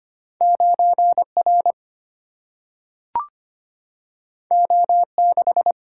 Morse code, heard as a keyed tone, about 25 wpm, 700 Hz. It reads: 9R O6
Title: Single-Letter Single-Number 25wpm